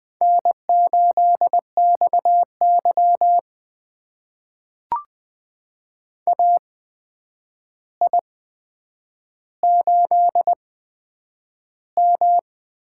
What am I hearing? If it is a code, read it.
Morse code, heard as a keyed tone, about 20 wpm, spaced as for 9 wpm, 700 Hz. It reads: N8XY AI8M
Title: USA Call Signs - 4 Characters - 8x Character Spacing - 20wpm